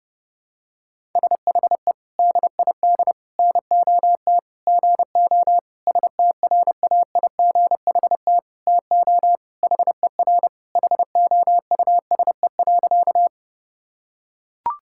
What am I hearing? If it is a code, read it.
Morse code, heard as a keyed tone, about 28 wpm, 700 Hz. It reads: SHE DID NOT GO STRAIGHT TO HER HOUSE.
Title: Sentences from Top 1000 Words - Review Part 6 - 30wpm